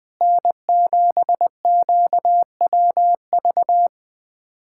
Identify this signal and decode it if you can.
Morse code, heard as a keyed tone, about 20 wpm, 700 Hz. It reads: N7QWV
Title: USA Callsign DE Callsign - 5 Characters - 20wpm